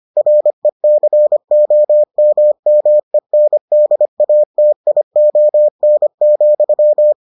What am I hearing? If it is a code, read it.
Morse code, heard as a keyed tone, about 25 wpm, 600 Hz. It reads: RECOMMENDATION,